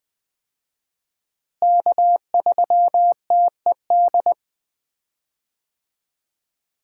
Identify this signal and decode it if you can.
Morse code, heard as a keyed tone, about 20 wpm, 700 Hz. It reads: K3TED
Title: USA Call Signs - 5 Characters - Repeated 2x - 20wpm